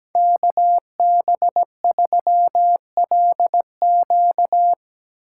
Morse code, heard as a keyed tone, about 19 wpm, 700 Hz. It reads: KB3LQ
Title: Speed Racing - USA 2x2 Call Signs 17wpm